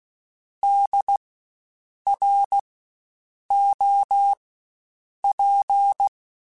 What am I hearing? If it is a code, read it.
Morse code, heard as a keyed tone, about 16 wpm, spaced as for 7 wpm, 800 Hz. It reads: DROP
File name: Spaced Alphas / Spaced Alphas mp3s / Ds sp @15